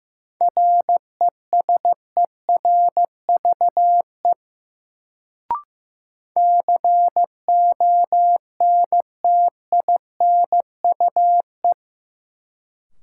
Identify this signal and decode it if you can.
Morse code, heard as a keyed tone, about 15 wpm, 700 Hz. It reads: RESERVE CONTINUE
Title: Sentences from Top 1000 Words - Review Part 6 - New Word Drill - 15wpm